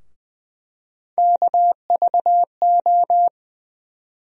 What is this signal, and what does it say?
Morse code, heard as a keyed tone, about 20 wpm, 700 Hz. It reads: KVO
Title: USA Call Sign Postfixes - 3 Letters 20wpm